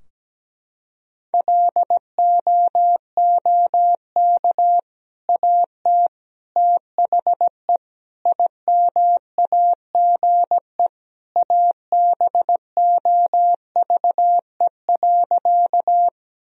Morse code, heard as a keyed tone, about 17 wpm, 700 Hz. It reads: LOOK AT THE IMAGE ABOVE.
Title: Sentences from Top 1000 Words - Review Part 4 - 17wpm